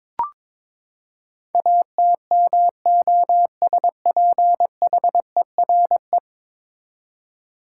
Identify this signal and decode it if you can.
Morse code, heard as a keyed tone, about 22 wpm, 700 Hz. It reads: ATMOSPHERE
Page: morse-code-ninja-practice-22wpm